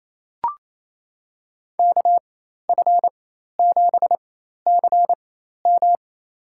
Morse code, heard as a keyed tone, about 28 wpm, spaced as for 13 wpm, 700 Hz. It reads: KF7CM
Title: USA Call Signs - 5 Characters - 4x Character Spacing - 28wpm